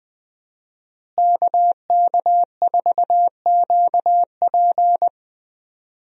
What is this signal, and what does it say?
Morse code, heard as a keyed tone, about 20 wpm, 700 Hz. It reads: KK4QP
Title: USA Call Signs - 5 Characters - Repeated 3x - 20wpm